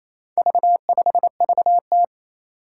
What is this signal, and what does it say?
Morse code, heard as a keyed tone, about 27 wpm, 700 Hz. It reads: V5VT
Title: Four Challenging Letter-Number Combinations 28wpm